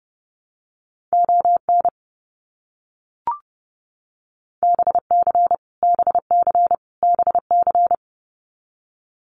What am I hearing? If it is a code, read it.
Morse code, heard as a keyed tone, about 30 wpm, 700 Hz. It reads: ON BC BC BC
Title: Canadian Provinces and Territories - Repeated 3x - 30wpm